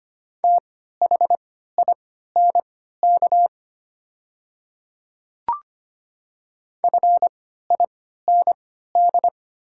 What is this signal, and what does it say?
Morse code, heard as a keyed tone, about 25 wpm, spaced as for 14 wpm, 690 Hz. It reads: THINK FIND
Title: 3x Character Spacing - Top 500 Words - 25wpm